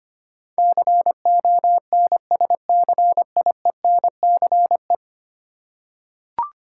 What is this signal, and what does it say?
Morse code, heard as a keyed tone, about 25 wpm, 700 Hz. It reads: CONSCIENCE
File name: Ten Letter Words - 25wpm